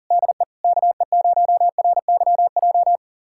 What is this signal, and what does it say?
Morse code, heard as a keyed tone, about 40 wpm, 700 Hz. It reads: DE KE0RYJ
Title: USA Callsign DE Callsign - 6 Characters - 40wpm